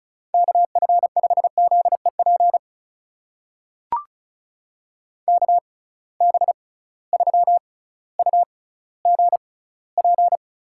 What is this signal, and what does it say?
Morse code, heard as a keyed tone, about 36 wpm, spaced as for 18 wpm, 700 Hz. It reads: KF5ZEP KB3UGP
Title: USA Call Signs - 6 Characters - 6x Character Spacing - 35wpm